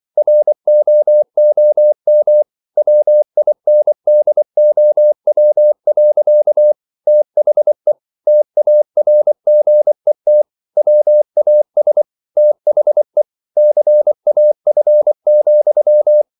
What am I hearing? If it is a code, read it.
Morse code, heard as a keyed tone, about 24 wpm, 600 Hz. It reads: ROOM WINDOW. THE TARGET WAS THE CAF,